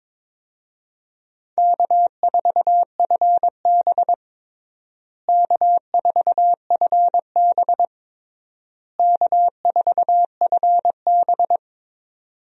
Morse code, heard as a keyed tone, about 22 wpm, 700 Hz. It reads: K4FB K4FB K4FB
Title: USA Call Signs - 4 Characters - Repeated 3x - 22wpm